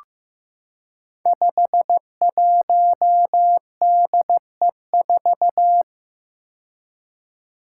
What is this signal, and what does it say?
Morse code, heard as a keyed tone, about 15 wpm, 700 Hz. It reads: 51DE4
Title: Five Challenging Letter-Number Combinations 15wpm